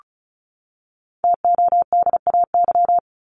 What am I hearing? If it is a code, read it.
Morse code, heard as a keyed tone, about 34 wpm, 700 Hz. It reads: TODAY
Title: Top 501-750 Words 35wpm